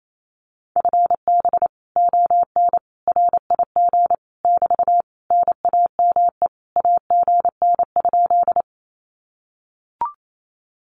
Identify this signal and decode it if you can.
Morse code, heard as a keyed tone, about 28 wpm, 700 Hz. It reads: FB ON RIG <BT> NAME AGN?
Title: Common QSO Phrases Encore 28wpm